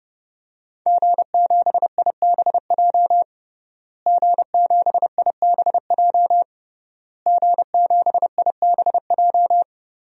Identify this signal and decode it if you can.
Morse code, heard as a keyed tone, about 30 wpm, 700 Hz. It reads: G7IBJ G7IBJ G7IBJ
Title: DXCC-100 Call Signs - 5 Characters - Repeated 3x - 30wpm